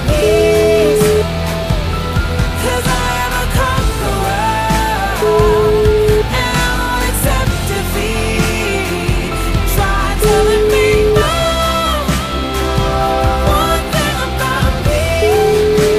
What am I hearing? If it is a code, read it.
Ringback tone (Europe, 425 Hz), with noise.